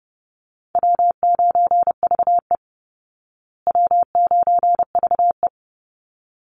Morse code, heard as a keyed tone, about 30 wpm, 700 Hz. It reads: W9VE W9VE
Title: Easy Call Signs Repeated 2x - 30wpm